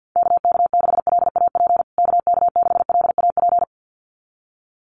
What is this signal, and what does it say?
Morse code, heard as a keyed tone, about 50 wpm, 700 Hz. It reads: KK6LAP KK6LAP
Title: USA Call Signs Repeated 2x 50wpm